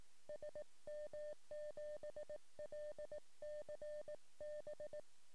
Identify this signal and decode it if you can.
Morse code, heard as a keyed tone, about 18 wpm, 600 Hz. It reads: SM7LCB
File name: sm7lcb / 08 sm5le-p jo98ax 23cm